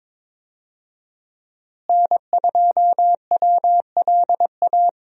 Morse code, heard as a keyed tone, about 22 wpm, 700 Hz. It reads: N2WLA
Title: USA Call Signs - 5 Characters - Repeated 2x - 22wpm